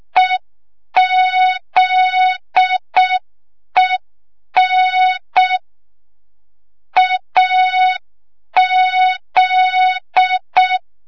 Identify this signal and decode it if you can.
Morse code, heard as a keyed tone, about 6 wpm, 3700 Hz. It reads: EZEN AZ
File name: Morze-HA2MN / hang / 25 GYAKORLAT